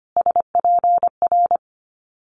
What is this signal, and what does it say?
Morse code, heard as a keyed tone, about 25 wpm, 700 Hz. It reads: SPR
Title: Multi-letter Phonemes Repeated 3x 25wpm